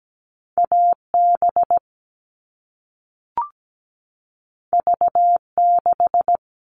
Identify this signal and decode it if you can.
Morse code, heard as a keyed tone, about 17 wpm, 700 Hz. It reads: AB V6
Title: All Cut Numbers - 2-Digit 17wpm